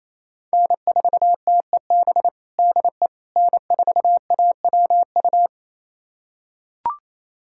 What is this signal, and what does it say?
Morse code, heard as a keyed tone, about 28 wpm, 700 Hz. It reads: N4TEB DE N4AWU
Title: USA Callsign DE Callsign - 5 Characters - 28wpm